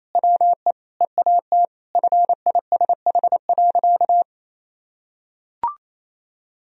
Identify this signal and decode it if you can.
Morse code, heard as a keyed tone, about 28 wpm, 700 Hz. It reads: WE EAT FISH.
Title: Sentences from Top 1000 Words - Review Part 1 - 28wpm